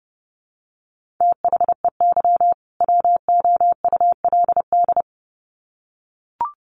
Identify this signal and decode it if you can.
Morse code, heard as a keyed tone, about 30 wpm, 700 Hz. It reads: THEY WOULD
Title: Sets of 2 Words 30wpm